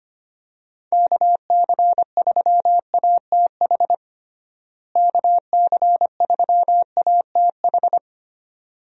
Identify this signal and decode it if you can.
Morse code, heard as a keyed tone, about 25 wpm, 700 Hz. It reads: KC3ATH KC3ATH
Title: USA Call Signs - 6 Characters - Repeated 2x - 25wpm